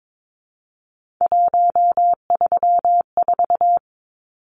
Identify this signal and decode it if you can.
Morse code, heard as a keyed tone, about 22 wpm, 700 Hz. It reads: 134
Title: Numbers - 3-Digit 22wpm